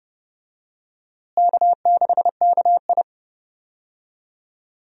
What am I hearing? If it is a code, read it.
Morse code, heard as a keyed tone, about 30 wpm, 700 Hz. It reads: K6KI
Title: USA Call Signs - 4 Characters - Repeated 3x - 30wpm